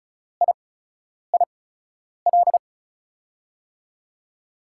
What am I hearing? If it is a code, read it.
Morse code, heard as a keyed tone, about 36 wpm, spaced as for 10 wpm, 700 Hz. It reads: IIL
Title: USA Call Signs - 5 Characters - 8x Character Spacing - 35wpm